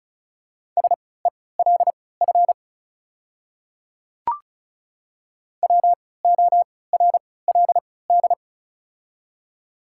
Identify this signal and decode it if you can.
Morse code, heard as a keyed tone, about 36 wpm, spaced as for 20 wpm, 700 Hz. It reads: SELF WORLD
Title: 3x Character Spacing - Top 200 Words - 35wpm